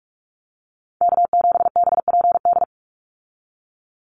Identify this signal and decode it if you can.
Morse code, heard as a keyed tone, about 44 wpm, 700 Hz. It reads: K7BPD
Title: USA Call Signs 45wpm